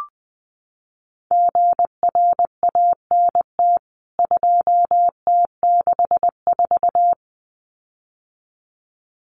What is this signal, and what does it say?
Morse code, heard as a keyed tone, about 20 wpm, 700 Hz. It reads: GRANT 2T64
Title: CWT Exchange 20wpm